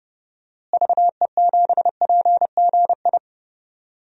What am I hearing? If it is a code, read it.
Morse code, heard as a keyed tone, about 30 wpm, 700 Hz. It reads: VE7PGI
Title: DXCC-100 Call Signs - 6 Characters - Repeated 2x - 30wpm